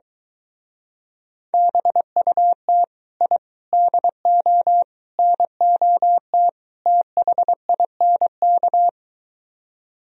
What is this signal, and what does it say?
Morse code, heard as a keyed tone, about 23 wpm, 700 Hz. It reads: BUT I DO NOT THINK